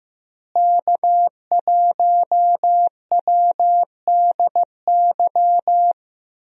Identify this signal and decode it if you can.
Morse code, heard as a keyed tone, about 15 wpm, 700 Hz. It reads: K1WDY
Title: USA Call Signs - 5 Characters - Repeated 2x - 15wpm